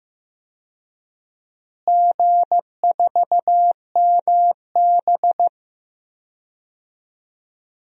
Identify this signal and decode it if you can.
Morse code, heard as a keyed tone, about 15 wpm, 700 Hz. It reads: G4MB
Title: DXCC-100 Call Signs - 4 Characters - Repeated 3x - 15wpm